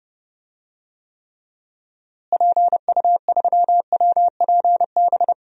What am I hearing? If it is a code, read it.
Morse code, heard as a keyed tone, about 31 wpm, 700 Hz. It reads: PU3WPB
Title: DXCC-100 Call Signs - 6 Characters - Repeated 3x - 30wpm